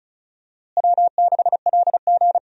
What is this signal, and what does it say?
Morse code, heard as a keyed tone, about 36 wpm, 700 Hz. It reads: W6LG
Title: USA Call Signs - 4 Characters - Repeated 3x - 35wpm